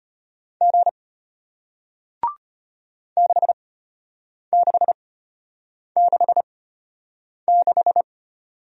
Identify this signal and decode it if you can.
Morse code, heard as a keyed tone, about 33 wpm, spaced as for 8 wpm, 700 Hz. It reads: G 6666